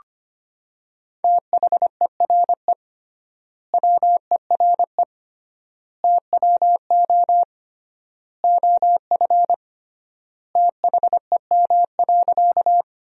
Morse code, heard as a keyed tone, about 25 wpm, 700 Hz. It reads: THERE WERE TWO OF THEM.
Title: 3x Word-Spacing - Sentences from Top 100 Words - 25wpm